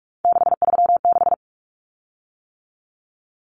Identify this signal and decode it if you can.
Morse code, heard as a keyed tone, about 44 wpm, 700 Hz. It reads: 636
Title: Common Cut Numbers - 3-Digit 45wpm